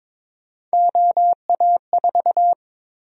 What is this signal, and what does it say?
Morse code, heard as a keyed tone, about 21 wpm, 700 Hz. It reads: OA4
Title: USA Call Sign Prefixes - Repeated 2x - 22wpm